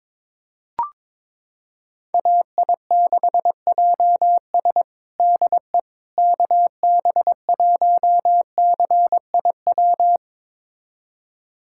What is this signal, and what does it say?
Morse code, heard as a keyed tone, about 22 wpm, 700 Hz. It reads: AI6JS DE KB1CIW
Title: USA Callsign DE Callsign 22wpm